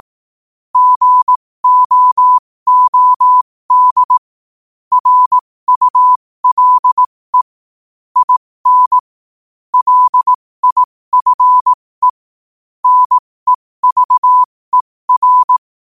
Morse code, heard as a keyed tone, about 18 wpm, spaced as for 15 wpm, 1000 Hz. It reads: GOOD RULE IN LIFE NEVER
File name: qotdfiles / 15wpm